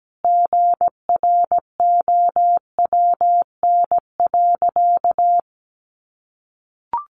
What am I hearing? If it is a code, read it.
Morse code, heard as a keyed tone, about 17 wpm, 700 Hz. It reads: GROWN.
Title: Easy Phrases 17wpm